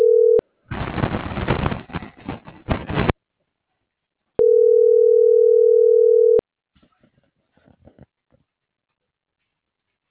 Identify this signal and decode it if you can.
Ringback tone (North America), 440 + 480 Hz.
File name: CENTRAL TERRESTRE / out claudiacruz - .75087